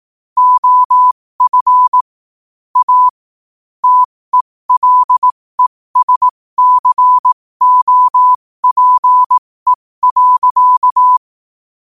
Morse code, heard as a keyed tone, about 18 wpm, spaced as for 15 wpm, 1000 Hz. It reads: OF A TELESCOPE.
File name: qotdfiles / 15wpm